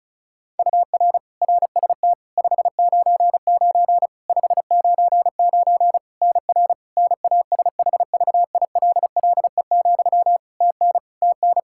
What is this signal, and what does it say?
Morse code, heard as a keyed tone, about 36 wpm, 700 Hz. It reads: UR RST 599 599 NR NASHVILLE, TN TN
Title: Practice QSO - Exchange 1 - Beginning - 35wpm